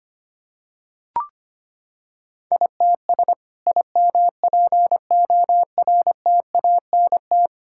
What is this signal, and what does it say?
Morse code, heard as a keyed tone, about 25 wpm, 700 Hz. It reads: ITS IMPORTANT
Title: Sentences from Top 1000 Words - Review Part 5 - 25wpm